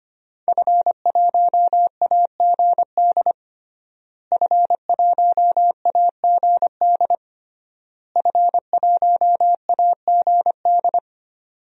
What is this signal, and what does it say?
Morse code, heard as a keyed tone, about 25 wpm, 700 Hz. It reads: F1AGD F1AGD F1AGD
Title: DXCC-100 Call Signs - 5 Characters - Repeated 3x - 25wpm